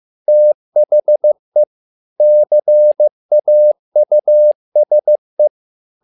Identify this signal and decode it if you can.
Morse code, heard as a keyed tone, about 15 wpm, 600 Hz. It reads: THE CAUSE
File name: learning center / tower trouble / tower trouble 15wpm